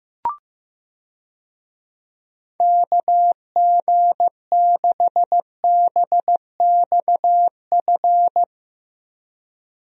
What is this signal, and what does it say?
Morse code, heard as a keyed tone, about 15 wpm, 700 Hz. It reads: KG6BXF
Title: USA Call Signs - 6 Characters - Repeated 3x - 15wpm